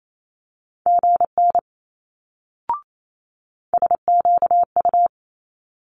Morse code, heard as a keyed tone, about 28 wpm, 700 Hz. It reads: GN SQU
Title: Multi-letter Phonemes Quick Fire 28wpm